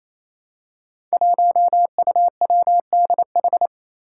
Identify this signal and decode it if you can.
Morse code, heard as a keyed tone, about 28 wpm, 700 Hz. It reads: 1UWDH
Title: Five Challenging Letter-Number Combinations 28wpm